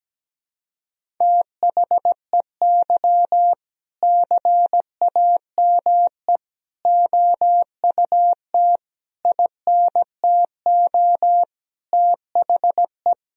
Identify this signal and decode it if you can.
Morse code, heard as a keyed tone, about 17 wpm, 700 Hz. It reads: THEY CAME OUT INTO THE
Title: Sentences from Top 1000 Words - Review Part 3 - 17wpm